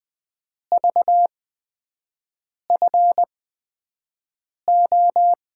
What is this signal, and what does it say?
Morse code, heard as a keyed tone, about 20 wpm, spaced as for 5 wpm, 700 Hz. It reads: VFO
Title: USA Call Signs - 5 Characters - 8x Character Spacing - 20wpm